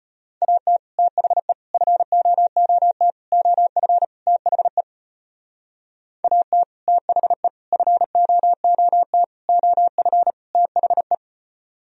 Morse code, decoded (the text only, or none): AT THE FOOT OF THE AT THE FOOT OF THE